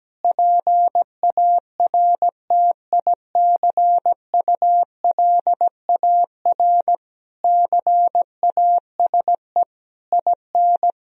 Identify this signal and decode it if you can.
Morse code, heard as a keyed tone, about 17 wpm, 700 Hz. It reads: PARTICULAR CASE IN
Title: Sentences from Top 1000 Words - Review Part 4 - 17wpm